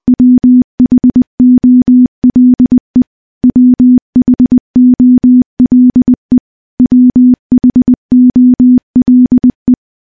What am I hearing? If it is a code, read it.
Morse code, heard as a keyed tone, about 21 wpm, 250 Hz. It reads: WHOLE WHOLE WHOLE
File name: morse-training-20WPM